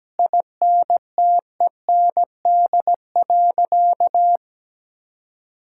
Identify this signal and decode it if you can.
Morse code, heard as a keyed tone, about 17 wpm, 700 Hz. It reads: INTEND.
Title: Sentences from Top 1600 Words 17wpm